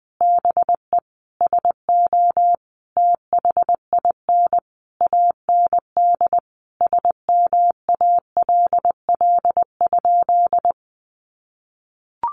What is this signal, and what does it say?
Morse code, heard as a keyed tone, about 20 wpm, 700 Hz. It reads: BE SO THIN AND SMALL?
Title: Sentences from Top 1200 Words 20wpm